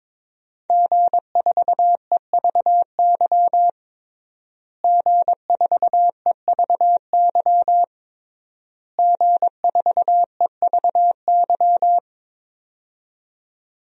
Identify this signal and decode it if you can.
Morse code, heard as a keyed tone, about 22 wpm, 700 Hz. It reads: G4EVY G4EVY G4EVY
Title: DXCC-100 Call Signs - 5 Characters - Repeated 3x - 22wpm